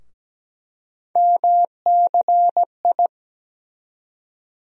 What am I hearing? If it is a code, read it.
Morse code, heard as a keyed tone, about 17 wpm, 700 Hz. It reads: MCI
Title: USA Call Sign Postfixes - 3 Letters 17wpm